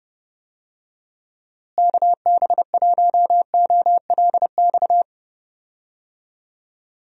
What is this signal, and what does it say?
Morse code, heard as a keyed tone, about 32 wpm, 700 Hz. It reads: KB1OLX
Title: USA Call Signs - 6 Characters - Repeated 2x - 30wpm